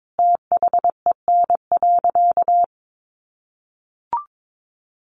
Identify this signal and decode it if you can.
Morse code, heard as a keyed tone, about 22 wpm, 700 Hz. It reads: THEN.
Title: Sentences from Top 200 Words 22wpm